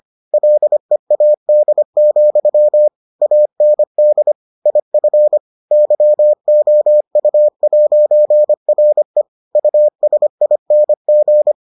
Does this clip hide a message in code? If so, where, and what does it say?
Yes: Morse code from 0.3 to 11.5 s, heard as a keyed tone, about 25 wpm, 600 Hz. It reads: LEAD, AND IF YOU'RE USING